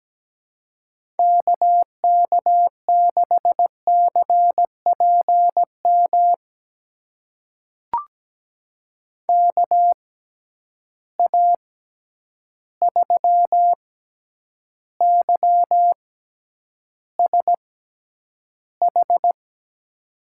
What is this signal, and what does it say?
Morse code, heard as a keyed tone, about 17 wpm, spaced as for 9 wpm, 700 Hz. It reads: KK6CPM KA3YSH
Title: USA Call Signs - 6 Characters - 6x Character Spacing - 17wpm